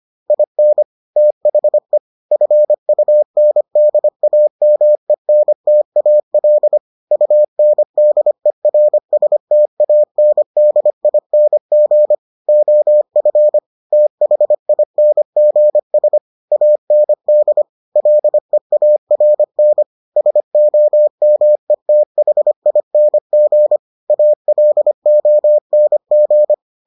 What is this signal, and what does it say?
Morse code, heard as a keyed tone, about 25 wpm, 600 Hz. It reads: IN THE FUNDAMENTAL UNDERSTANDING OF THINGS AND LEARN SOMETHING ALONG